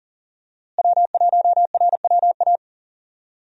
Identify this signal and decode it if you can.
Morse code, heard as a keyed tone, about 37 wpm, 700 Hz. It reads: W1RWA
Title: USA Call Signs - 5 Characters - Repeated 2x - 40wpm